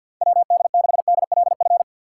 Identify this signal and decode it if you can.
Morse code, heard as a keyed tone, about 53 wpm, 700 Hz. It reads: WD6DLF
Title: USA Call Signs - 6 Characters - 8x Character Spacing - 50wpm